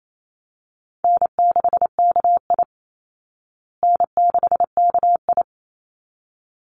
Morse code, heard as a keyed tone, about 28 wpm, 700 Hz. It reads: N6KI N6KI
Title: Call Signs Repeated 3x - 28wpm